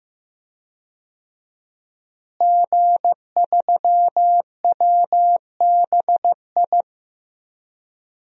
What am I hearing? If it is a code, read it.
Morse code, heard as a keyed tone, about 15 wpm, 700 Hz. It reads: G3WBI
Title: DXCC-100 Call Signs - 5 Characters - Repeated 3x - 15wpm